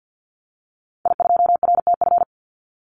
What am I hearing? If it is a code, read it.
Morse code, heard as a keyed tone, about 50 wpm, 700 Hz. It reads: I2RTF
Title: Call Signs - 50wpm